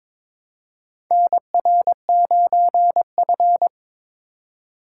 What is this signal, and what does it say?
Morse code, heard as a keyed tone, about 22 wpm, 700 Hz. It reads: NR9F
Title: USA Call Signs - 4 Characters - 4x Character Spacing - 22wpm